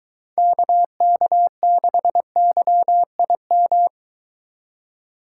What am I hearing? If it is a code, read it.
Morse code, heard as a keyed tone, about 23 wpm, 700 Hz. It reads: KK6YIM